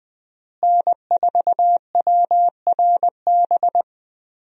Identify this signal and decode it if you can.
Morse code, heard as a keyed tone, about 21 wpm, 700 Hz. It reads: N4WRB
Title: USA Call Signs - 5 Characters - Repeated 3x - 20wpm